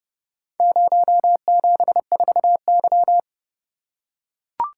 Morse code, heard as a keyed tone, about 31 wpm, 700 Hz. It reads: 074Y